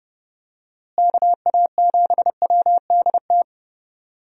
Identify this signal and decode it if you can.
Morse code, heard as a keyed tone, about 32 wpm, 700 Hz. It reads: KA7WDT
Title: USA Call Signs - 6 Characters - Repeated 3x - 30wpm